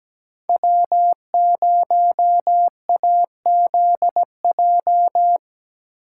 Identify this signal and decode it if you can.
Morse code, heard as a keyed tone, about 19 wpm, 700 Hz. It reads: W0AZJ
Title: Speed Racing - USA 1x3 Call Signs 17wpm